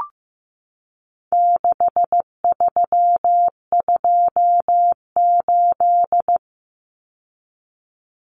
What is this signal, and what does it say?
Morse code, heard as a keyed tone, about 15 wpm, 700 Hz. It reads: 6328
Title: Common Cut Numbers - 4-Digit 15wpm